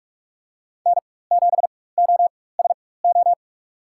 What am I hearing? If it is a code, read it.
Morse code, heard as a keyed tone, about 45 wpm, spaced as for 21 wpm, 700 Hz. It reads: N7OSO
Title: USA Call Signs - 5 Characters - 4x Character Spacing - 45wpm